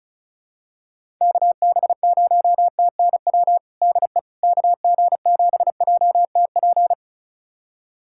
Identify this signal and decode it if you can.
Morse code, heard as a keyed tone, about 34 wpm, 700 Hz. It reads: KB0TNW DE KG7JTP
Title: USA Callsign DE Callsign - 6 Characters - 35wpm